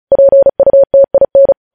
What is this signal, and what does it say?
Morse code, heard as a keyed tone, about 36 wpm, 550 Hz. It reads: PUTIN
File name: podcast / News headlines-1x-35x35